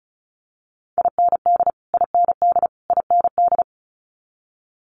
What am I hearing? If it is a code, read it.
Morse code, heard as a keyed tone, about 36 wpm, 700 Hz. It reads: IND IND IND
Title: Multi-letter Phonemes Repeated 3x 35wpm